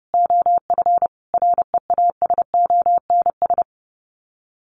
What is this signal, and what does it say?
Morse code, heard as a keyed tone, about 30 wpm, 700 Hz. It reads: OF REASONS
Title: Sets of 5 Words Encore 30wpm